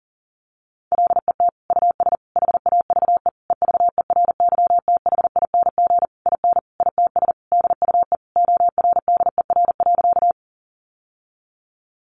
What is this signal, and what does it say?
Morse code, heard as a keyed tone, about 40 wpm, 700 Hz. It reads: LET US HAVE EVERYTHING IN ITS DUE ORDER.
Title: Sentences from Top 900 Words 40wpm